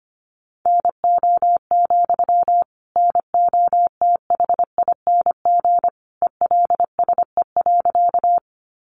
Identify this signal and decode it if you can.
Morse code, heard as a keyed tone, about 25 wpm, 700 Hz. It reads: NO, NOTHING ELSE.
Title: Sentences from Top 800 Words 25wpm